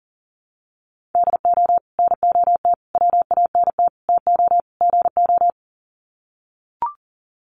Morse code, heard as a keyed tone, about 40 wpm, 700 Hz. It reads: DO NOT WANT TO GO
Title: Sets of 5 Words 40wpm